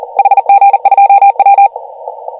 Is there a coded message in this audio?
Yes: Morse code from 0.2 to 1.7 s, heard as a keyed tone, about 43 wpm, 800 Hz. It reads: HG2W